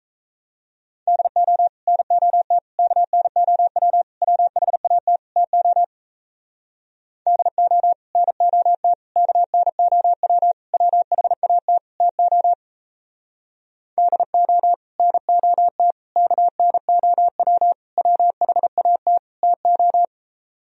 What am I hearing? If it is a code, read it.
Morse code, heard as a keyed tone, about 37 wpm, 700 Hz. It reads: DO NOT KNOW WHAT TO DO NOT KNOW WHAT TO DO NOT KNOW WHAT TO